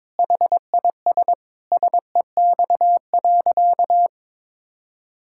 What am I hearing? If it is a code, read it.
Morse code, heard as a keyed tone, about 22 wpm, 700 Hz. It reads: HIS SEX.
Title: Sentences from Top 1600 Words 22wpm